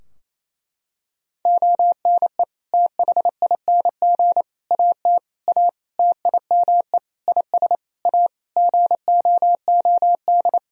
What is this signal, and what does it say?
Morse code, heard as a keyed tone, about 28 wpm, 700 Hz. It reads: ONE THING AT A TIME IS A GOOD
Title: Sentences from Top 1000 Words - Review Part 2 - 28wpm